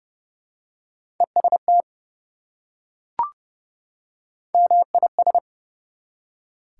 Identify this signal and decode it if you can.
Morse code, heard as a keyed tone, about 30 wpm, 700 Hz. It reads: EST MIS
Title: Prefixes and Suffixes 30wpm